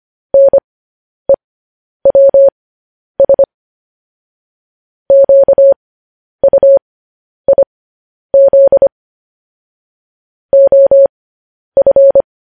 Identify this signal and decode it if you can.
Morse code, heard as a keyed tone, about 25 wpm, spaced as for 10 wpm, 550 Hz. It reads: NEWS QUIZ OF
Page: news headlines in morse code at 10 wpm